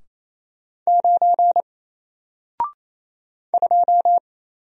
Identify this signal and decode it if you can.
Morse code, heard as a keyed tone, about 29 wpm, 700 Hz. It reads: 9 2
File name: Single-Letter-Number-28wpm